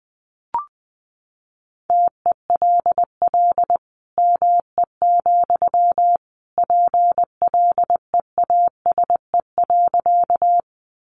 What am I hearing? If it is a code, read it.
Morse code, heard as a keyed tone, about 20 wpm, 700 Hz. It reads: TELL ME, PLEASE.
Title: Sentences from Top 1500 Words 20wpm